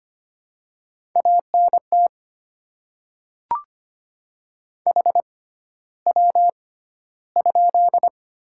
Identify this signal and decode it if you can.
Morse code, heard as a keyed tone, about 25 wpm, spaced as for 13 wpm, 700 Hz. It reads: ANT HW?
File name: Top 20 QSO Elements - 6x Character Spacing - 25wpm